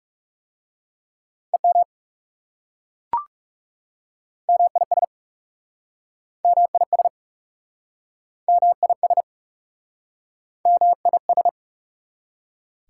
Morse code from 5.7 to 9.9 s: MIS MIS

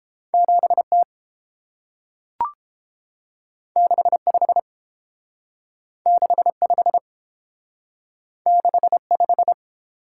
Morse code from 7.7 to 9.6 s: 6 5

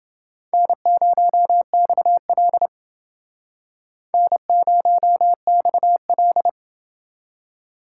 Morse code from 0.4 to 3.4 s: N0XL

